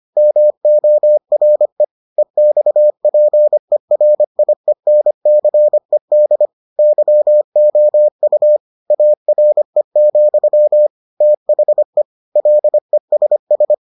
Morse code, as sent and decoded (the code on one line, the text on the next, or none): -- --- .-. / . -..- .--. .-. .. -. -.-. -.. / -.-- --- ..- / .- .-. --..-- / - .... / .-.. ... ...
MORE EXPERIENCED YOU ARE, THE LESS